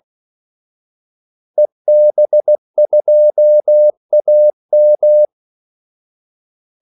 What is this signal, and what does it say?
Morse code, heard as a keyed tone, about 16 wpm, 600 Hz. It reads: EB2AM
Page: Kaldesignaler 16 wpm